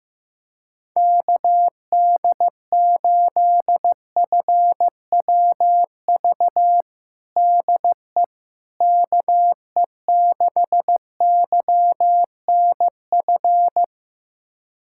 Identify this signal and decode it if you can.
Morse code, heard as a keyed tone, about 15 wpm, 700 Hz. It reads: KD8FWV DE KE6YNF
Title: USA Callsign DE Callsign 15wpm